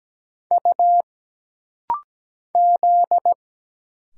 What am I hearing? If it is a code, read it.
Morse code, heard as a keyed tone, about 17 wpm, 700 Hz. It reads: U Z
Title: Single Letters - Rapid-Fire 17wpm